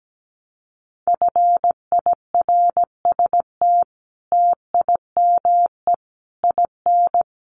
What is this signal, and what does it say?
Morse code, heard as a keyed tone, about 17 wpm, 700 Hz. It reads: FIRST TIME IN
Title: Sets of 5 Words Encore2 17wpm